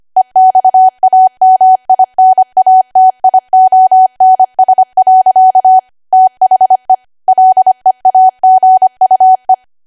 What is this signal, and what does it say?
Morse code, heard as a keyed tone, about 25 wpm, 750 Hz. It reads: EXAMINATIONS. THE LEAGUE